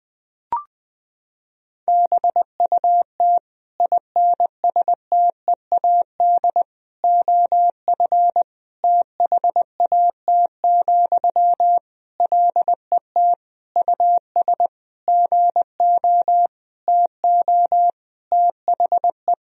Morse code, heard as a keyed tone, about 20 wpm, 700 Hz. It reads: BUT INSTEAD OF THAT, LET US GO TO THE
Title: Sentences from Top 1000 Words - Review Part 4 - 20wpm